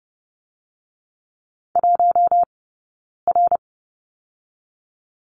Morse code, heard as a keyed tone, about 30 wpm, 700 Hz. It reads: 1 R
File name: Lesson 19d - Letter w - Two Characters 30wpm